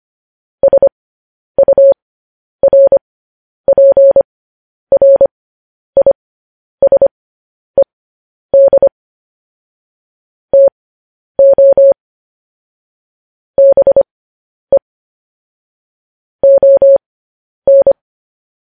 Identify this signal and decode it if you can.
Morse code, heard as a keyed tone, about 25 wpm, spaced as for 10 wpm, 550 Hz. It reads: SURPRISED TO BE ON
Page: news headlines in morse code at 10 wpm